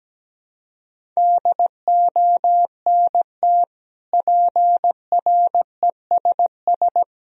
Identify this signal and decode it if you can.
Morse code, heard as a keyed tone, about 17 wpm, 700 Hz. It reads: DONT PRESS
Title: Sentences from Top 1000 Words - Review Part 1 - 17wpm